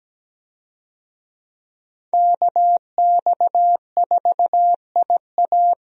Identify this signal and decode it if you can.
Morse code, heard as a keyed tone, about 17 wpm, 700 Hz. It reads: KX4IA
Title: USA Call Signs - 5 Characters - Repeated 3x - 17wpm